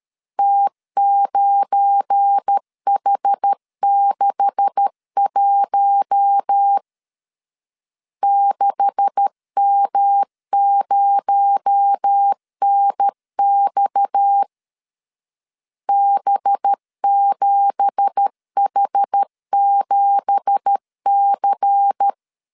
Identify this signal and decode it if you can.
Morse code, heard as a keyed tone, about 13 wpm, 800 Hz. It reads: T9H61 6M0NX B7H7C